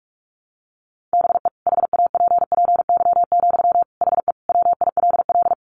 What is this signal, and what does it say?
Morse code, heard as a keyed tone, about 44 wpm, 700 Hz. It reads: BE HAPPY, HE WILL
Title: Sentences from Top 1200 Words 45wpm